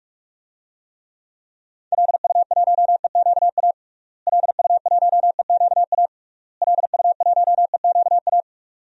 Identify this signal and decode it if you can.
Morse code, heard as a keyed tone, about 41 wpm, 700 Hz. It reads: LU1EQA LU1EQA LU1EQA
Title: DXCC-100 Call Signs - 6 Characters - Repeated 3x - 45wpm